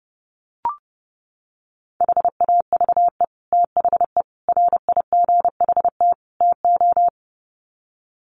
Morse code, heard as a keyed tone, about 29 wpm, 700 Hz. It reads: HAVE THE RIGHT TO
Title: Sets of 4 Words 30wpm